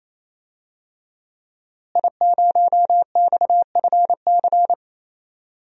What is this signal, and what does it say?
Morse code, heard as a keyed tone, about 27 wpm, 700 Hz. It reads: I0XFC
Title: DXCC-100 Call Signs - 5 Characters - Repeated 3x - 28wpm